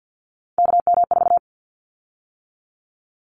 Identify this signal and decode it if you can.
Morse code, heard as a keyed tone, about 50 wpm, 700 Hz. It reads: KM4
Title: Callsign Prefixes for Top 25 DXCC - 50wpm